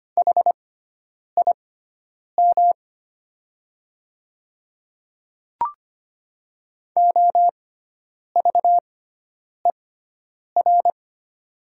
Morse code, heard as a keyed tone, about 25 wpm, spaced as for 9 wpm, 700 Hz. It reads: HIM OVER
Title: 6x Character Spacing - Top 100 Words - 25wpm